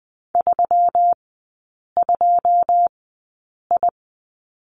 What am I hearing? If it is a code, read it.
Morse code, heard as a keyed tone, about 20 wpm, 700 Hz. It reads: 3 2 I
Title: 121 Lesson 20e - Number 3 - Three Characters 20wpm